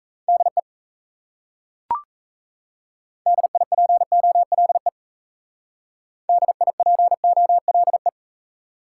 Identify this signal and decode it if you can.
Morse code, heard as a keyed tone, about 36 wpm, 700 Hz. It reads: DE DIPOLE DIPOLE